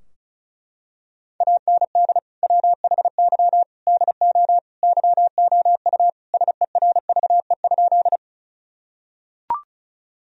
Morse code, heard as a keyed tone, about 36 wpm, 700 Hz. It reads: AND WHY DO YOU SERVE?
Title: Sentences from Top 1000 Words - Review Part 2 - 35wpm